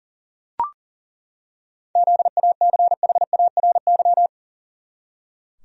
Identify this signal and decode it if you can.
Morse code, heard as a keyed tone, about 40 wpm, 700 Hz. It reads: ZACHARY
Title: USA Common Names - 40wpm